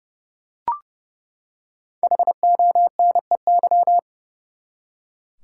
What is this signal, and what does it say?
Morse code, heard as a keyed tone, about 31 wpm, 700 Hz. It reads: HONEY
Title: Five Letter Words - 30wpm